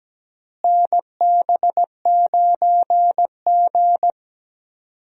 Morse code, heard as a keyed tone, about 17 wpm, 700 Hz. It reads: NB9G
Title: USA Call Signs - 4 Characters - Repeated 3x - 17wpm